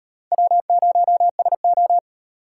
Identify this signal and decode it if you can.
Morse code, heard as a keyed tone, about 38 wpm, 700 Hz. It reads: W0SO